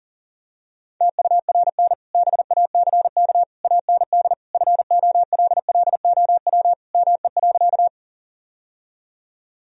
Morse code, heard as a keyed tone, about 40 wpm, 700 Hz. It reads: TURN BACK AND FOLLOW ME.
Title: Sentences from Top 1000 Words - Review Part 1 - 40wpm